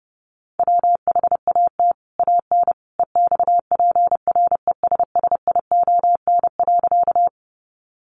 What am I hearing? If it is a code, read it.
Morse code, heard as a keyed tone, about 30 wpm, 700 Hz. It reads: WHAT AN EXPRESSION.
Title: Sentences from Top 1200 Words 30wpm